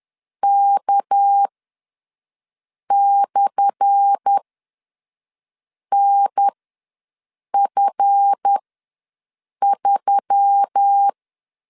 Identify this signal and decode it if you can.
Morse code, heard as a keyed tone, about 10 wpm, spaced as for 5 wpm, 800 Hz. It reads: K/NF3
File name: LEZIONI / 10 SIGNS / 10A SLASH ws 10wpm 4121